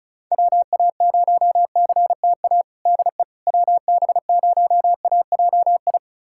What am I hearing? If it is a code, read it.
Morse code, heard as a keyed tone, about 34 wpm, 700 Hz. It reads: WA0CTA DE WB0AJI
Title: USA Callsign DE Callsign - 6 Characters - 35wpm